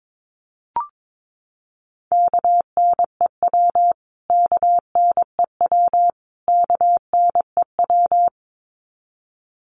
Morse code, heard as a keyed tone, about 22 wpm, 700 Hz. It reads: KNEW KNEW KNEW
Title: Top 500 Words Repeated 22wpm